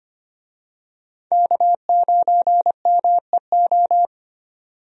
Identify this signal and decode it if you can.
Morse code, heard as a keyed tone, about 25 wpm, 700 Hz. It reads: K9MEO
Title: USA Call Signs - 5 Characters - 25wpm